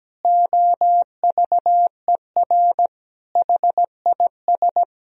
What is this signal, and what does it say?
Morse code, heard as a keyed tone, about 17 wpm, 700 Hz. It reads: OVER HIS
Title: Sentences from Top 1000 Words - Review Part 3 - 17wpm